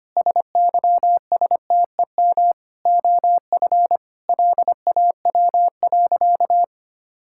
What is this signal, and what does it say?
Morse code, heard as a keyed tone, about 25 wpm, 700 Hz. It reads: SYSTEM OF LAW.
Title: Sentences from Top 1000 Words - Review Part 5 - 25wpm